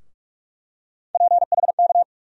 Morse code, heard as a keyed tone, about 42 wpm, 700 Hz. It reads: PHK
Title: USA Call Sign Postfixes - 3 Letters 45wpm